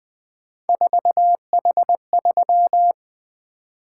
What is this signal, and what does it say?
Morse code, heard as a keyed tone, about 20 wpm, 700 Hz. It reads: 4H3
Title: Three Letter-Number Combinations - 20wpm